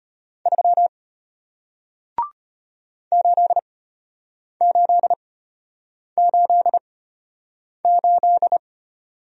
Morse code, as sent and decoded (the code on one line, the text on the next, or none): ...-- / ---.. ---.. ---.. ---..
3 8888